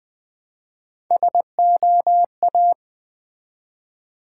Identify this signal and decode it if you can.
Morse code, heard as a keyed tone, about 20 wpm, 700 Hz. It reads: SOA
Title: Build A Word - 1 Letter At A Time - 3 to 5-Letter Words - 20wpm